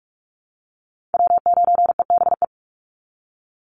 Morse code, heard as a keyed tone, about 44 wpm, 700 Hz. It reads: W9EBE
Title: Call Signs Repeated 3x - 45wpm